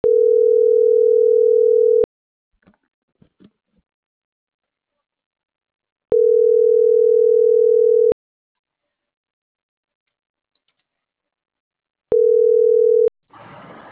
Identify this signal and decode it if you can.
Ringback tone (North America), 440 + 480 Hz.
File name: CENTRAL MAYAB / out israel.zuniga - .29254